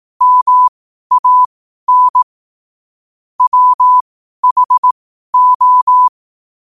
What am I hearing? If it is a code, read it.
Morse code, heard as a keyed tone, about 18 wpm, spaced as for 12 wpm, 1000 Hz. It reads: MAN WHO
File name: qotdfiles / 12wpm